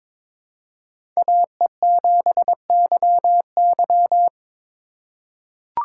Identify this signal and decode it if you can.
Morse code, heard as a keyed tone, about 22 wpm, 700 Hz. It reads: AE7YY
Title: USA Call Signs - 5 Characters - 2x Character Spacing - 22wpm